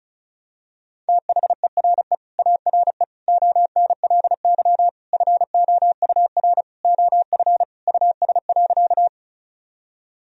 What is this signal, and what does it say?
Morse code, heard as a keyed tone, about 34 wpm, 700 Hz. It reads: THERE ARE ONLY FOUR OF US.
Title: Sentences from Top 1000 Words - Review Part 1 - 35wpm